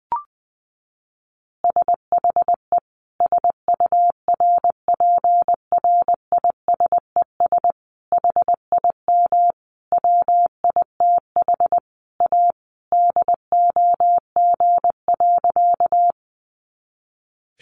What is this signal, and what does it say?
Morse code, heard as a keyed tone, about 20 wpm, 700 Hz. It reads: SHE SURPRISES HIM WITH A DOG.
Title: Additional Easy Phrases 20wpm